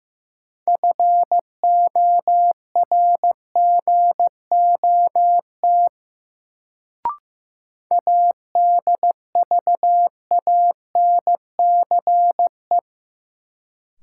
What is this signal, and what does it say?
Morse code, heard as a keyed tone, about 15 wpm, 700 Hz. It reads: FORGOT ADVANCE
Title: Sentences from Top 1400 Words - New Word Drill Alt - 15wpm